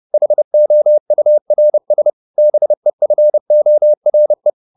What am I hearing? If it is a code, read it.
Morse code, heard as a keyed tone, about 30 wpm, 600 Hz. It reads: HOURS BEFORE
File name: the old man and the sea / the old man and the sea 1 30wpm